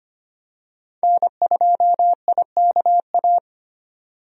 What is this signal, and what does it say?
Morse code, heard as a keyed tone, about 25 wpm, 690 Hz. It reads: N2IKA
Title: USA Call Signs - 5 Characters - 25wpm